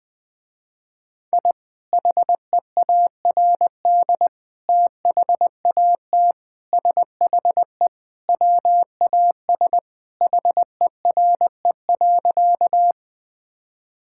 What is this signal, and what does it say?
Morse code, heard as a keyed tone, about 21 wpm, 700 Hz. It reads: I HEARD THAT SHE WAS HERE.
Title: Sentences with 3-Word Phrases - Spoken Beginning - 20wpm